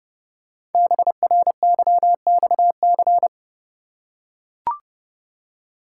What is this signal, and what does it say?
Morse code, heard as a keyed tone, about 30 wpm, 700 Hz. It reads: BRYXC
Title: Code Groups - 5 Letters - 30wpm